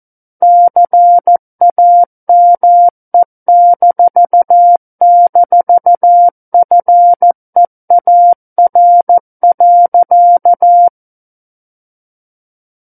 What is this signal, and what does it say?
Morse code, heard as a keyed tone, about 14 wpm, 700 Hz. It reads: CAME--FEAR.
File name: Book One / 05-Chapter 5 - 14 WPM